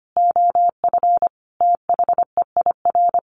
Morse code, heard as a keyed tone, about 25 wpm, 700 Hz. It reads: OF THEIR
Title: Sets of 5 Words Encore 25wpm